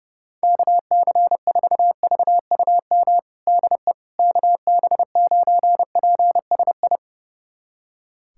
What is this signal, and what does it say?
Morse code, heard as a keyed tone, about 30 wpm, 700 Hz. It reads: KC4VUM DE KB9PSI
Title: USA Callsign DE Callsign - 6 Characters - 30wpm